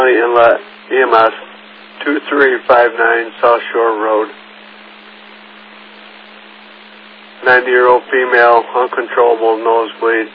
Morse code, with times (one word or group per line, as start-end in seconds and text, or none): none